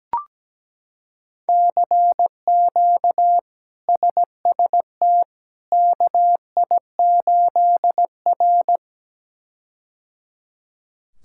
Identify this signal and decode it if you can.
Morse code, heard as a keyed tone, about 17 wpm, 700 Hz. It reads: CQ SST KI8R
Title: K1USN SST QSO Practice - 17wpm